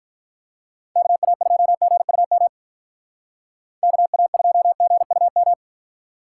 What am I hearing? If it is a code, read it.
Morse code, heard as a keyed tone, about 44 wpm, 700 Hz. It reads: KA2GUM KA2GUM